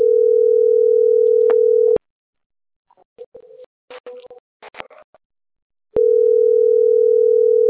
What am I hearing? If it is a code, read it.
Ringback tone (North America), 440 + 480 Hz.